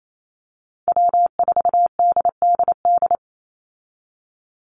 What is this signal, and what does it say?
Morse code, heard as a keyed tone, about 28 wpm, 700 Hz. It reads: W4DDD
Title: Call Signs Repeated 2x - 28wpm